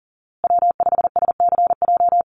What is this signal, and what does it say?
Morse code, heard as a keyed tone, about 40 wpm, 700 Hz. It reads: W5SCJ
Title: USA Call Signs Repeated 2x 40wpm